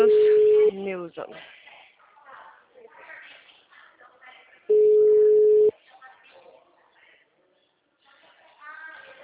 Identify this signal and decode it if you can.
Ringback tone (Europe, 425 Hz), with speech.